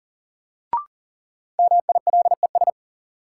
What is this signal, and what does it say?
Morse code, heard as a keyed tone, about 40 wpm, 700 Hz. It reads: MILES
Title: Sentences from Top 1000 Words - Review Part 3 - New Word Drill - 40wpm